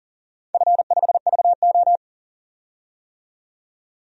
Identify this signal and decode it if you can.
Morse code, heard as a keyed tone, about 38 wpm, 700 Hz. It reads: F5VO
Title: DXCC-100 Call Signs - 4 Characters - 40wpm